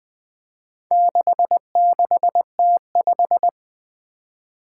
Morse code, heard as a keyed tone, about 20 wpm, 700 Hz. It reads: 66T5
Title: Four Challenging Letter-Number Combinations 20wpm